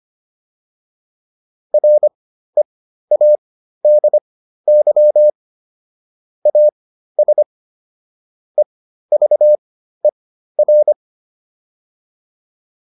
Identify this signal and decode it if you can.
Morse code, heard as a keyed tone, about 25 wpm, spaced as for 13 wpm, 600 Hz. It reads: READY AS EVER
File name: PR405 13